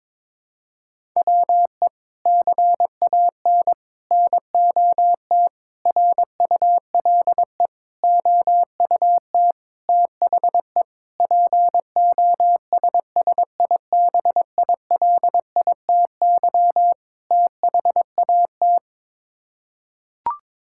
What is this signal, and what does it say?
Morse code, heard as a keyed tone, about 22 wpm, 700 Hz. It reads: WE CAN NOT RULE OUT THE POSSIBILITY THAT
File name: practice / 22wpm / Sets of 8 Words - 22wpm